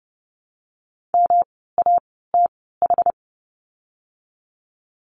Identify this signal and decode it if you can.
Morse code, heard as a keyed tone, about 31 wpm, spaced as for 17 wpm, 700 Hz. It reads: MATH
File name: course / 30:10wpm / 108 Lesson 18f - Letter m - Words 30wpm